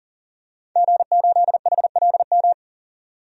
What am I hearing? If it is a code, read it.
Morse code, heard as a keyed tone, about 40 wpm, 700 Hz. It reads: G8HLM
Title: DXCC-100 Call Signs - 5 Characters - 40wpm